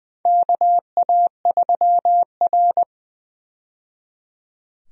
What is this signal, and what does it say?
Morse code, heard as a keyed tone, about 20 wpm, 700 Hz. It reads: KA3R
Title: USA Call Signs - 4 Characters - 20wpm